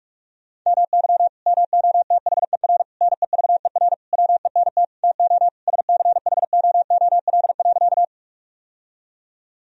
Morse code, heard as a keyed tone, about 41 wpm, 700 Hz. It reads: MY MOTHER NEVER WENT TO SCHOOL.
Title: Sentences from Top 1000 Words - Review Part 1 - 45wpm